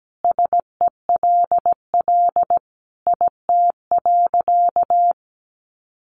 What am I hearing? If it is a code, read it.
Morse code, heard as a keyed tone, about 17 wpm, 700 Hz. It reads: SELL IT.
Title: Sentences from Top 1100 Words 17wpm